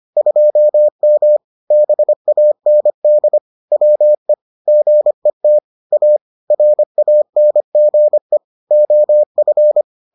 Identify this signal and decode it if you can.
Morse code, heard as a keyed tone, about 25 wpm, 600 Hz. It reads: 2M BAND WE GET A RANGE OF